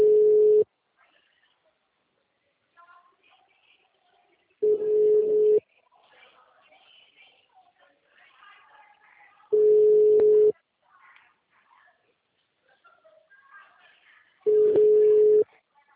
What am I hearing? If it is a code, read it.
Ringback tone (Europe, 425 Hz), with speech.